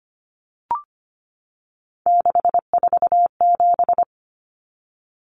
Morse code, heard as a keyed tone, about 25 wpm, 690 Hz. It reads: 647
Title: Numbers - 3-Digit 25wpm